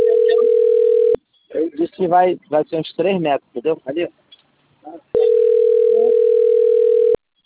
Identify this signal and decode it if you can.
Ringback tone (North America), 440 + 480 Hz, with speech.
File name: out- - - - - .617